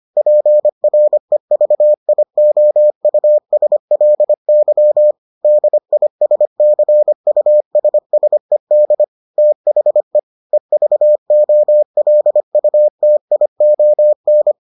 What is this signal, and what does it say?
Morse code, heard as a keyed tone, about 25 wpm, 600 Hz. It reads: PREVIOUSLY DISCUSSED THE EVOLUTION